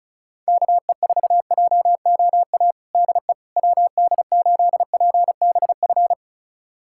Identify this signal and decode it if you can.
Morse code, heard as a keyed tone, about 34 wpm, 700 Hz. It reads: KE4JOA DE WD8PBF
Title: USA Callsign DE Callsign - 6 Characters - 35wpm